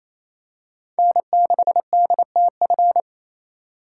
Morse code, heard as a keyed tone, about 28 wpm, 700 Hz. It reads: N6DTF
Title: USA Call Signs - 5 Characters - 2x Character Spacing - 28wpm